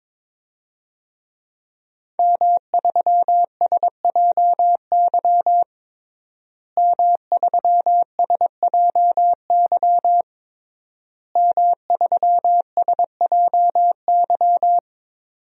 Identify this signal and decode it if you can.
Morse code, heard as a keyed tone, about 22 wpm, 700 Hz. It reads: M3SJY M3SJY M3SJY
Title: DXCC-100 Call Signs - 5 Characters - Repeated 3x - 22wpm